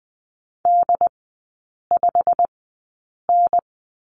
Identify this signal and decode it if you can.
Morse code, heard as a keyed tone, about 20 wpm, spaced as for 8 wpm, 700 Hz. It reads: D5N
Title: 093 Lesson 16e - Letter c - Three Characters 20wpm